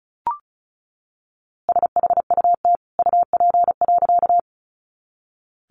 Morse code, heard as a keyed tone, about 34 wpm, 700 Hz. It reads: SHUT UP.
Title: A Sound Of Thunder 35wpm